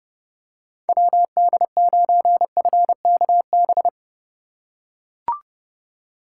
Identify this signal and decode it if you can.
Morse code, heard as a keyed tone, about 30 wpm, 700 Hz. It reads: WD9FKB